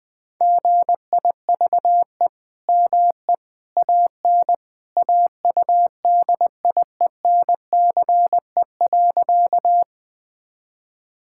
Morse code, heard as a keyed tone, about 20 wpm, 700 Hz. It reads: GIVE ME AN AUDIENCE.
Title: Sentences from Top 1000 Words - Review Part 5 - 20wpm